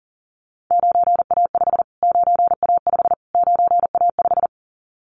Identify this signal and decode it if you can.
Morse code, heard as a keyed tone, about 40 wpm, 700 Hz. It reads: 9A5 9A5 9A5
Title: Callsign Prefixes for Top 25 DXCC - Repeated 3x - 40wpm